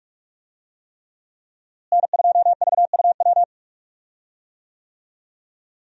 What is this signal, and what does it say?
Morse code, heard as a keyed tone, about 46 wpm, 700 Hz. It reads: N2VUW
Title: USA Call Signs - 5 Characters - Repeated 3x - 45wpm